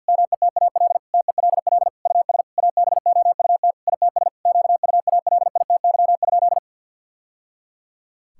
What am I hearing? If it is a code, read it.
Morse code, heard as a keyed tone, about 50 wpm, 700 Hz. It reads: METAL TELL US ABOUT ITS QUALITY?
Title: Sentences from Top 1000 Words - Review Part 6 - 50wpm